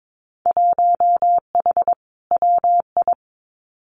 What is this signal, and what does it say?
Morse code, heard as a keyed tone, about 22 wpm, 700 Hz. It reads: 1H WI
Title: Winter Field Day Exchanges 22wpm